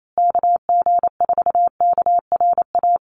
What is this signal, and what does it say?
Morse code, heard as a keyed tone, about 28 wpm, 700 Hz. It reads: KG4KRA
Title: USA Call Signs Repeated 2x 28wpm